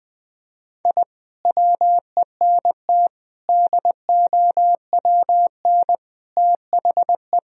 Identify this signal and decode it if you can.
Morse code, heard as a keyed tone, about 21 wpm, 700 Hz. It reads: I WENT DOWN THE
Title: Sentences from Top 1000 Words - Review Part 1 - 20wpm